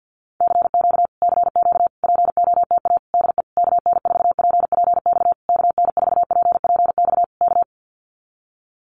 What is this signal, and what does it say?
Morse code, heard as a keyed tone, about 50 wpm, 700 Hz. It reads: CQ CQ POTA DE KN4PPX KN4PPX K
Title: Common QSO Phrases Encore 50wpm